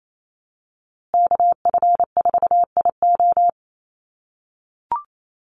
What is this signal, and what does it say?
Morse code, heard as a keyed tone, about 28 wpm, 700 Hz. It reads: KF4IO
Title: USA Call Signs Repeated 2x 28wpm